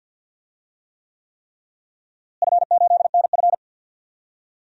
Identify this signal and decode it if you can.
Morse code, heard as a keyed tone, about 48 wpm, 700 Hz. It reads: F8NF
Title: DXCC-100 Call Signs - 4 Characters - Repeated 3x - 50wpm